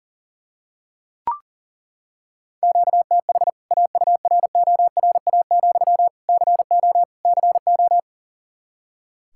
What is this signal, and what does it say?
Morse code, heard as a keyed tone, about 40 wpm, 700 Hz. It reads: QTH AURORA, CO CO
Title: K1USN SST QSO Practice 40wpm